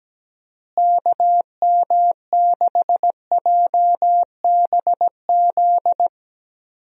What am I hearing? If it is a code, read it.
Morse code, heard as a keyed tone, about 17 wpm, 700 Hz. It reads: KM6JBZ
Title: USA Call Signs - 6 Characters - Repeated 2x - 17wpm